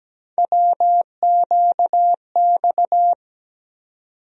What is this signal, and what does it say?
Morse code, heard as a keyed tone, about 17 wpm, 700 Hz. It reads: WQX
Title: USA Call Sign Postfixes - 3 Letters 17wpm